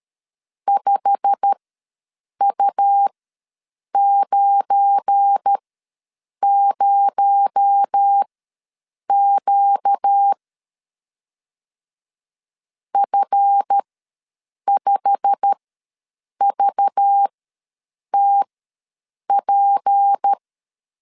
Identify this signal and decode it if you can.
Morse code, heard as a keyed tone, about 13 wpm, spaced as for 7 wpm, 800 Hz. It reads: 5U90Q F5VTP
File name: LEZIONI / 8 YJ45 / 8 YJ45 ws 12wpm 3146